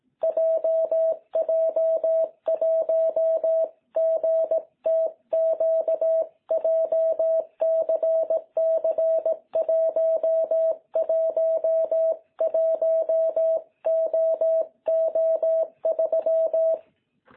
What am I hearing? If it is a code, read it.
Morse code, heard as a keyed tone, about 18 wpm, 660 Hz. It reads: JJ1GTQ JCC111OO3
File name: jh8cbh / 2021cw